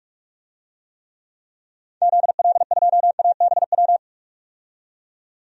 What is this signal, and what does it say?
Morse code, heard as a keyed tone, about 46 wpm, 700 Hz. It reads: ZL2ABW
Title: DXCC-100 Call Signs - 6 Characters - 45wpm